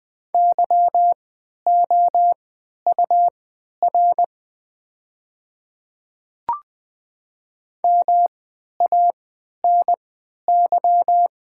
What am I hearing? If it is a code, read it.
Morse code, heard as a keyed tone, about 20 wpm, spaced as for 11 wpm, 700 Hz. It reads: YOUR MANY
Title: 3x Character Spacing - Top 200 Words - 20wpm